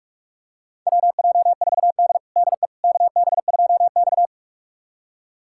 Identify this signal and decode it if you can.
Morse code, heard as a keyed tone, about 42 wpm, 700 Hz. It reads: WJ4D DE KB2X
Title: USA Callsign DE Callsign - 4 Characters - 45wpm